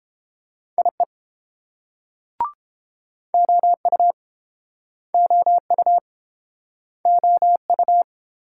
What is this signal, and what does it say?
Morse code, heard as a keyed tone, about 32 wpm, 700 Hz. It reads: IE OU OU OU